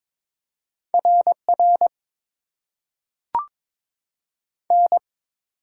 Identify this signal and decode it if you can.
Morse code, heard as a keyed tone, about 22 wpm, 700 Hz. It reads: R R N
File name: Top 90 QSO Elements - 8x Character Spacing - 22wpm